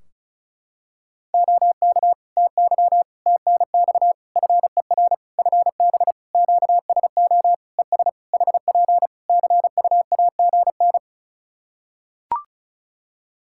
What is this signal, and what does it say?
Morse code, heard as a keyed tone, about 36 wpm, 700 Hz. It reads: OK TY TNX FER FB QSO ES HP CUAGN
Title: K1USN SST QSO Practice 35wpm